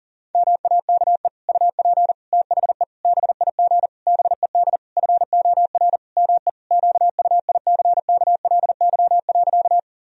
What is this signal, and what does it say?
Morse code, heard as a keyed tone, about 40 wpm, 700 Hz. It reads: MAKE UP THE BIG BED FOR ME QUICKLY.
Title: Sentences from Top 1000 Words - Review Part 1 - 40wpm